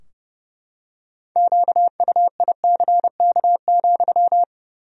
Morse code, heard as a keyed tone, about 30 wpm, 700 Hz. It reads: QUICK,
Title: 3x Word-Spacing - Sentences from Top 500 Words - 30wpm